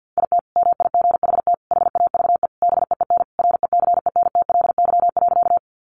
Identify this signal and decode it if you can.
Morse code, heard as a keyed tone, about 50 wpm, 700 Hz. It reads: IT MIGHT HAVE BEEN RECENTLY.
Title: Sentences from Top 1200 Words 50wpm